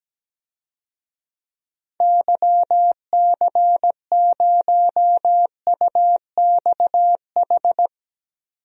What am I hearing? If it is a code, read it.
Morse code, heard as a keyed tone, about 17 wpm, 700 Hz. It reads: YC0UXH
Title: DXCC-100 Call Signs - 6 Characters - 17wpm